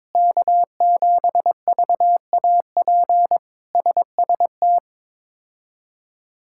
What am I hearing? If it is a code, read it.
Morse code, heard as a keyed tone, about 21 wpm, 700 Hz. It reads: K7VAP SST
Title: K1USN SST QSO Practice - 22wpm